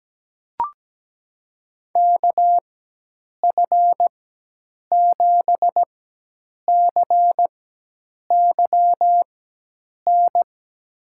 Morse code, heard as a keyed tone, about 17 wpm, spaced as for 8 wpm, 700 Hz. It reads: KF7CYN
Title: USA Call Signs - 6 Characters - 4x Character Spacing - 17wpm